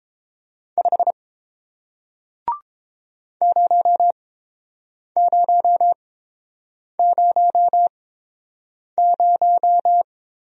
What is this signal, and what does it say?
Morse code, heard as a keyed tone, about 31 wpm, spaced as for 7 wpm, 700 Hz. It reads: 5 0000